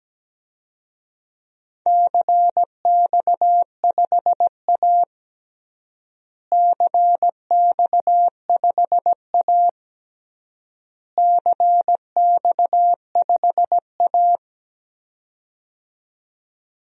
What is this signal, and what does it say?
Morse code, heard as a keyed tone, about 17 wpm, 700 Hz. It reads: CX5A CX5A CX5A
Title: DXCC-100 Call Signs - 4 Characters - Repeated 3x - 17wpm